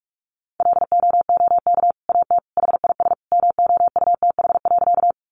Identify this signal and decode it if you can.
Morse code, heard as a keyed tone, about 44 wpm, 700 Hz. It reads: LOOK AT HIS MOUTH.
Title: Sentences from Top 1500 Words 45wpm